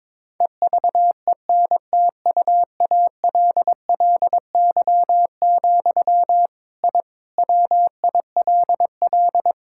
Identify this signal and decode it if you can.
Morse code, heard as a keyed tone, about 21 wpm, 700 Hz. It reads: EVENTUALLY, I WILL
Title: Sentences from Top 1000 Words - Review Part 6 - 22wpm